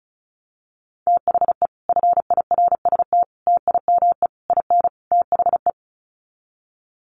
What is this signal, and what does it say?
Morse code, heard as a keyed tone, about 34 wpm, 700 Hz. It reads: THE FIRST TIME IN THE
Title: Sets of 5 Words Encore 35wpm